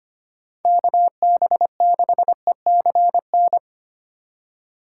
Morse code, heard as a keyed tone, about 25 wpm, 700 Hz. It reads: KB6ECN
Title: USA Call Signs - 6 Characters - 3x Character Spacing - 25wpm